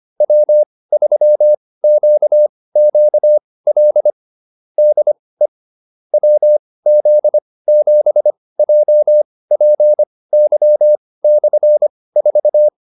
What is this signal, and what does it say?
Morse code, heard as a keyed tone, about 25 wpm, spaced as for 18 wpm, 600 Hz. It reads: W3QQL DE WZ7JPY/4